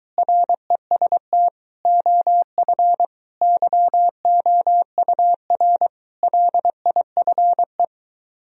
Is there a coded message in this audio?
Yes: Morse code from 0.2 to 7.8 s, heard as a keyed tone, about 23 wpm, 700 Hz. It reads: REST OF YOUR LIFE